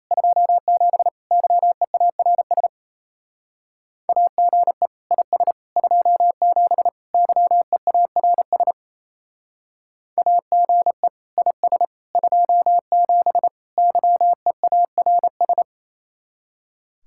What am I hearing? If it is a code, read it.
Morse code, heard as a keyed tone, about 35 wpm, 700 Hz. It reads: 27 YEARS AGE IS 27 YEARS AGE IS 27 YEARS
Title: Speed-Racing Common QSO Phrases - 28wpm